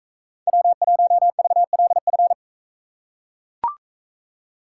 Morse code, heard as a keyed tone, about 40 wpm, 700 Hz. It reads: W1VLF